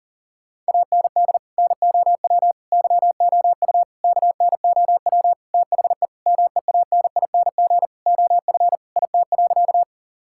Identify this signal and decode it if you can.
Morse code, heard as a keyed tone, about 37 wpm, 700 Hz. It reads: AND NOW YOU KNOW THE MEANING OF IT.
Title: Sentences from Top 1000 Words - Review Part 6 - 40wpm